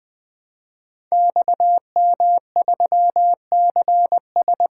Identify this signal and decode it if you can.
Morse code, heard as a keyed tone, about 20 wpm, 700 Hz. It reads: XM3CS
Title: Code Groups - 5 Letter:Numbers - 20wpm